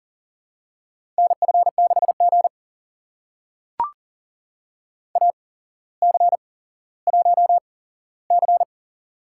Morse code, heard as a keyed tone, about 38 wpm, spaced as for 17 wpm, 700 Hz. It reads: NF6G AC1C
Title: USA Call Signs - 4 Characters - 8x Character Spacing - 40wpm